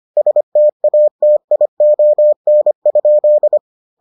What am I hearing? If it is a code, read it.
Morse code, heard as a keyed tone, about 25 wpm, 600 Hz. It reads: STATION?